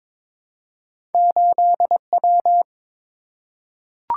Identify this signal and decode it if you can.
Morse code, heard as a keyed tone, about 22 wpm, 700 Hz. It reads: 8W
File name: practice / 22wpm / Two Letter-Number Combinations - 22wpm